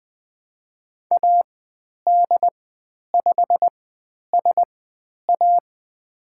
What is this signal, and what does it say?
Morse code, heard as a keyed tone, about 20 wpm, spaced as for 10 wpm, 700 Hz. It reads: AD5SA
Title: 102 Lesson 17g - Letter u - Call Signs 20wpm